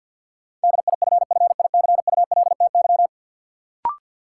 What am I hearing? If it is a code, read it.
Morse code, heard as a keyed tone, about 53 wpm, 700 Hz. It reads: DIFFICULTY
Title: Sentences from Top 1500 Words - New Word Drill - 50wpm